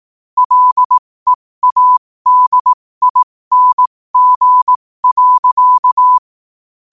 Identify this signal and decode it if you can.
Morse code, heard as a keyed tone, about 18 wpm, spaced as for 15 wpm, 1000 Hz. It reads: LEADING.
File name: qotdfiles / 15wpm